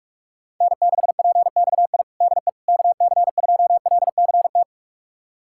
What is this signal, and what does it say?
Morse code, heard as a keyed tone, about 42 wpm, 700 Hz. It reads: N6PXI DE KC2LCT
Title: USA Callsign DE Callsign 45wpm